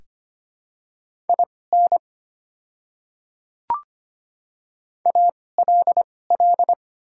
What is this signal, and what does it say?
Morse code, heard as a keyed tone, about 25 wpm, spaced as for 18 wpm, 690 Hz. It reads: IN ALL
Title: 2x Character Spacing - Top 100 Words - 25wpm